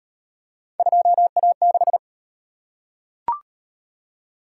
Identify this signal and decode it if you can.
Morse code, heard as a keyed tone, about 35 wpm, 700 Hz. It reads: 2A6